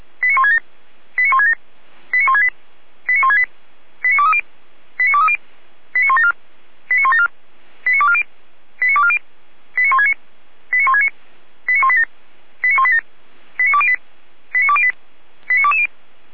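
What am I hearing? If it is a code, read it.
Five-tone selective call (ZVEI), heard as quick runs of five tones, tones of about 80 ms. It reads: 89167 89167 89168 89168 89230 89230 89174 89174 89240 89240 89169 89169 89187 89187 89208 89208 89220
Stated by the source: ZVEI five-tone selective call 89121, received on SDR.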